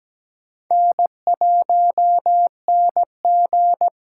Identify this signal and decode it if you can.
Morse code, heard as a keyed tone, about 17 wpm, 700 Hz. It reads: N1NG
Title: USA Call Signs - 8x Character Spacing - 17wpm